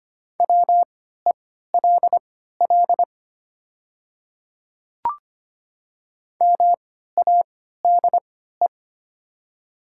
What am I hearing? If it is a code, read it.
Morse code, heard as a keyed tone, about 25 wpm, spaced as for 14 wpm, 700 Hz. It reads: WELL MADE
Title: 3x Character Spacing - Top 200 Words - 25wpm